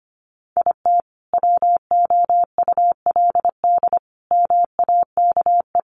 Morse code, heard as a keyed tone, about 25 wpm, 700 Hz. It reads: IT WOULD MAKE
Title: Sentences from Top 1100 Words 25wpm